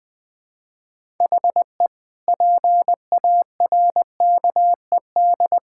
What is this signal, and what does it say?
Morse code, heard as a keyed tone, about 20 wpm, 700 Hz. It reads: HE PARKED
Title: Sentences with 4-Word Phrases - 20wpm